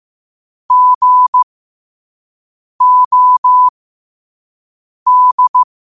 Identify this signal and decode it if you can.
Morse code, heard as a keyed tone, about 15 wpm, spaced as for 5 wpm, 1000 Hz. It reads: GOD
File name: qotdfiles / 5wpm